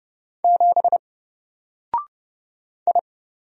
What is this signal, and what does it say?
Morse code, heard as a keyed tone, about 30 wpm, 700 Hz. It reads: 7 I
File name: Single-Letter-Number-30wpm